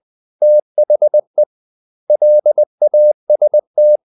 Morse code, heard as a keyed tone, about 20 wpm, 600 Hz. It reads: THE LAST
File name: the old man and the sea part 2 20wpm